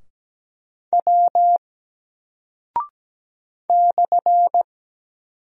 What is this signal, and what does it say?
Morse code, heard as a keyed tone, about 17 wpm, 700 Hz. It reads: W /
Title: Single Character (Letter:Number:Punctuation:Prosign) Alt 17wpm